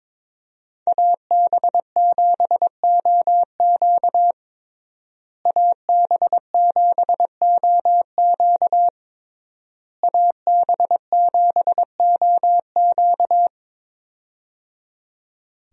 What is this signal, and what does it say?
Morse code, heard as a keyed tone, about 21 wpm, 700 Hz. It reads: AB7OQ AB7OQ AB7OQ
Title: USA Call Signs - 5 Characters - Repeated 3x - 22wpm